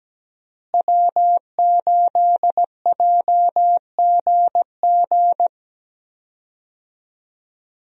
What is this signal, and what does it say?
Morse code, heard as a keyed tone, about 17 wpm, 700 Hz. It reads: W8JGG
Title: USA Call Signs - 5 Characters - Repeated 3x - 17wpm